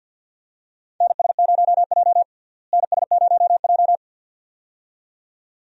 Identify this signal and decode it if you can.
Morse code, heard as a keyed tone, about 49 wpm, 700 Hz. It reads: NS0J NS0J
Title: USA Call Signs - 4 Characters - Repeated 2x - 50wpm